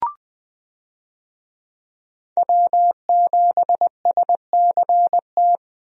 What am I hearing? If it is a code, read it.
Morse code, heard as a keyed tone, about 20 wpm, 700 Hz. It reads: W7SCT
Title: USA Call Signs - 5 Characters - 20wpm